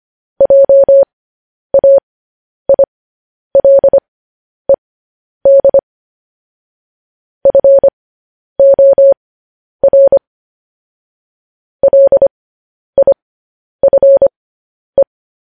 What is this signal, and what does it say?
Morse code, heard as a keyed tone, about 25 wpm, spaced as for 10 wpm, 550 Hz. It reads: JAILED FOR LIFE
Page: news headlines in morse code at 10 wpm